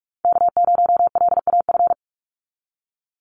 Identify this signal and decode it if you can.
Morse code, heard as a keyed tone, about 44 wpm, 700 Hz. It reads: K0LAF
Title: NA SOTA Chaser Callsigns 45wpm